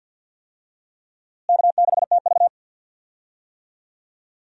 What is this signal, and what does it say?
Morse code, heard as a keyed tone, about 53 wpm, 700 Hz. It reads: K6TV
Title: USA Call Signs - 4 Characters - Repeated 3x - 50wpm